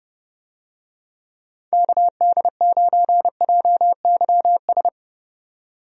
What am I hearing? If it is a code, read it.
Morse code, heard as a keyed tone, about 30 wpm, 700 Hz. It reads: KD9JYS
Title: USA Call Signs - 6 Characters - 30wpm